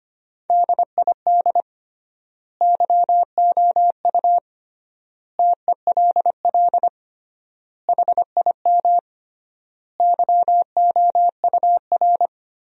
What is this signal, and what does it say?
Morse code, heard as a keyed tone, about 25 wpm, 700 Hz. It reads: DID YOU TELL HIM YOUR
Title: 3x Word-Spacing - Sentences from Top 400 Words - 25wpm